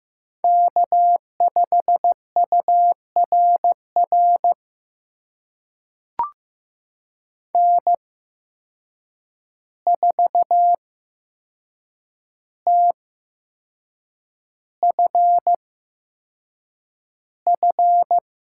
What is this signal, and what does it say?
Morse code, heard as a keyed tone, about 15 wpm, spaced as for 6 wpm, 700 Hz. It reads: K5URR N4TFF
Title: USA Call Signs - 5 Characters - 8x Character Spacing - 15wpm